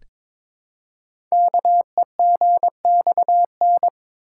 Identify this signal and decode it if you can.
Morse code, heard as a keyed tone, about 22 wpm, 700 Hz. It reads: KEGXN
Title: Code Groups - 5 Letters - 22wpm